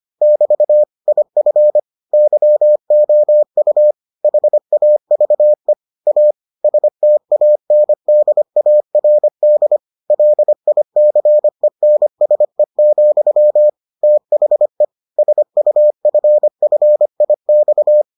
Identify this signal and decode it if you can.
Morse code, heard as a keyed tone, about 25 wpm, 600 Hz. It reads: <BT> IF YOU HAVE A STANDARD LICENSE, THE SUFFIX